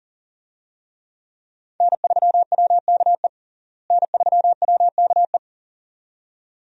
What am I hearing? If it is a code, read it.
Morse code, heard as a keyed tone, about 40 wpm, 700 Hz. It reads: N3WKE N3WKE
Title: USA Call Signs - 5 Characters - Repeated 2x - 40wpm